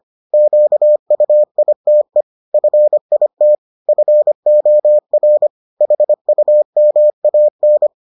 Morse code, heard as a keyed tone, about 25 wpm, 600 Hz. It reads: QUITE FIT FOR HUMAN